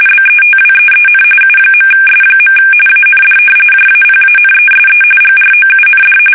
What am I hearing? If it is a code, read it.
RTTY, about 45.45 baud. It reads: BETWEEN U.S. AND JAPANESE GROUND
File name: fsk850